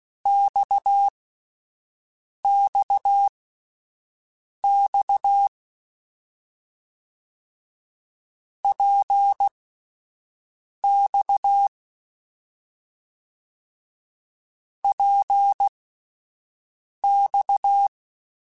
Morse code, heard as a keyed tone, about 16 wpm, spaced as for 5 wpm, 800 Hz. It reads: XXX PX PX
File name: Short Runs / Short Runs-mp3s / Balanced @10